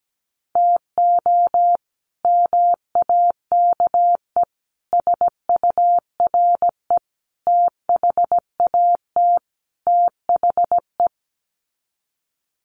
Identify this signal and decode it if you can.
Morse code, heard as a keyed tone, about 17 wpm, 700 Hz. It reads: TO MAKE SURE THAT THE
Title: Sets of 5 Words 17wpm